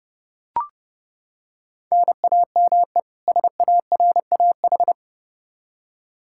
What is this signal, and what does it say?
Morse code, heard as a keyed tone, about 30 wpm, 700 Hz. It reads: NAME SARAH
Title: K1USN SST QSO Practice 30wpm